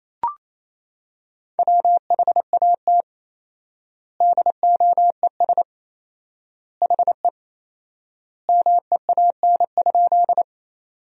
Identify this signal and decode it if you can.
Morse code, heard as a keyed tone, about 28 wpm, 700 Hz. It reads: WHAT DOES HE MEAN?
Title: 4x Word-Spacing - Sentences from Top 200 Words - 28wpm